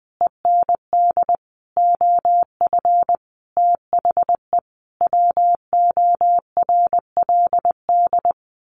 Morse code, heard as a keyed tone, about 19 wpm, 700 Hz. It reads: END OF THE WORLD
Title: Sets of 5 Words 20wpm